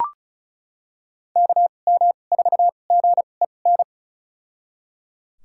Morse code, heard as a keyed tone, about 34 wpm, spaced as for 25 wpm, 700 Hz. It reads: KM4GEN
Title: USA Call Signs - 6 Characters - 2x Character Spacing - 35wpm